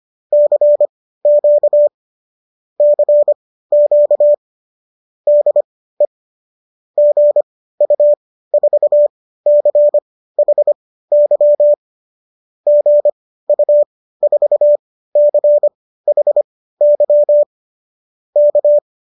Morse code, heard as a keyed tone, about 25 wpm, spaced as for 15 wpm, 570 Hz. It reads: CQ CQ DE GU4CHY GU4CHY K